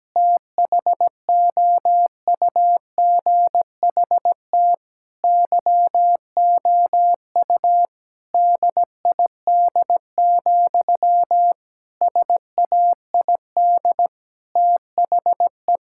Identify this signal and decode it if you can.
Morse code, heard as a keyed tone, about 17 wpm, 700 Hz. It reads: THOUGHT YOU DID, SAID THE
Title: Sentences from Top 1000 Words - Review Part 5 - 17wpm